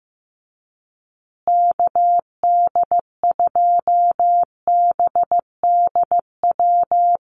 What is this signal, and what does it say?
Morse code, heard as a keyed tone, about 14 wpm, 700 Hz. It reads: KD2BDW
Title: USA Call Signs Repeated 3x 15wpm